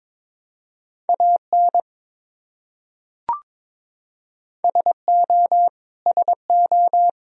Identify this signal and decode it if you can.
Morse code, heard as a keyed tone, about 22 wpm, 700 Hz. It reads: AN SO SO
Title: Top 100 Words Repeated 22wpm